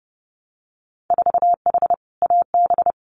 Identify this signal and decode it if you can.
Morse code, heard as a keyed tone, about 30 wpm, 700 Hz. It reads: 4H AB
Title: Winter Field Day Exchanges Encore 30wpm